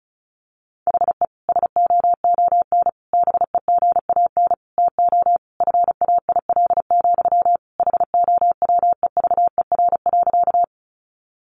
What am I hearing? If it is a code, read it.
Morse code, heard as a keyed tone, about 34 wpm, 700 Hz. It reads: HE SOON BEGAN TO FAIL, HOWEVER.
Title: Sentences from Top 1100 Words 35wpm